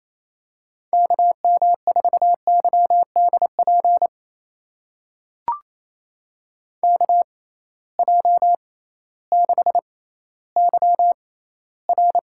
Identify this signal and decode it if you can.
Morse code, heard as a keyed tone, about 28 wpm, spaced as for 15 wpm, 700 Hz. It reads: KM4YDP KJ6YR
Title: USA Call Signs - 6 Characters - 6x Character Spacing - 28wpm